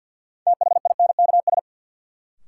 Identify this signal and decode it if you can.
Morse code, heard as a keyed tone, about 48 wpm, 700 Hz. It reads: THINKS
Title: Sentences from Top 1000 Words - Review Part 6 - New Word Drill - 50wpm